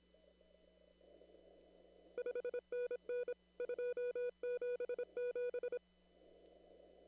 Morse code, heard as a keyed tone, about 26 wpm, 490 Hz. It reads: 5NN 277
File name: CW / kt5j wpx cw-op-k5pi